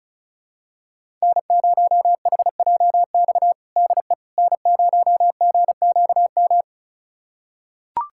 Morse code, heard as a keyed tone, about 34 wpm, 700 Hz. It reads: N0HJX DE N0GQM
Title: USA Callsign DE Callsign - 5 Characters - 35wpm